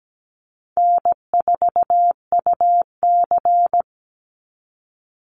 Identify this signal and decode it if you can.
Morse code, heard as a keyed tone, about 17 wpm, 700 Hz. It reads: N4UC
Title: Call Signs - 17wpm